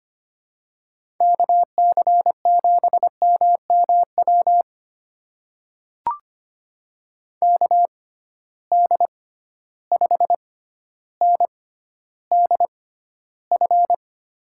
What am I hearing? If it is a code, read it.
Morse code, heard as a keyed tone, about 25 wpm, spaced as for 13 wpm, 700 Hz. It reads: KC7MMW KD5NDF
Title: USA Call Signs - 6 Characters - 6x Character Spacing - 25wpm